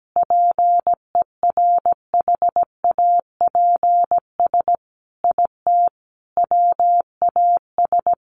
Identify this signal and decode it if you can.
Morse code, heard as a keyed tone, about 17 wpm, 700 Hz. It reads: PERHAPS IT WAS
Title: Sentences from Top 500 Words 17wpm